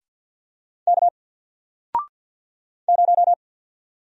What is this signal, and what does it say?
Morse code, heard as a keyed tone, about 48 wpm, 700 Hz. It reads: K 0
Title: Single Letter-Number Alt 50wpm